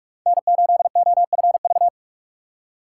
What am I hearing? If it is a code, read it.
Morse code, heard as a keyed tone, about 44 wpm, 700 Hz. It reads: N8OFV